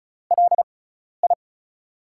Morse code, heard as a keyed tone, about 36 wpm, spaced as for 12 wpm, 700 Hz. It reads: LI
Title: USA Call Signs - 4 Characters - 6x Character Spacing - 35wpm